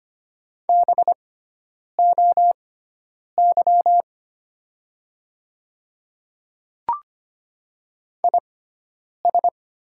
Morse code, heard as a keyed tone, about 25 wpm, spaced as for 9 wpm, 700 Hz. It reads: BOY IS
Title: 6x Character Spacing - Top 200 Words - 25wpm